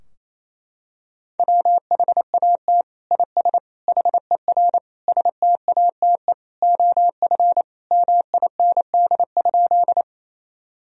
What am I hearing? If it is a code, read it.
Morse code, heard as a keyed tone, about 28 wpm, 700 Hz. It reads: WHAT IS HER STATE OF MIND?
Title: Sentences from Top 1000 Words - Review Part 2 - 28wpm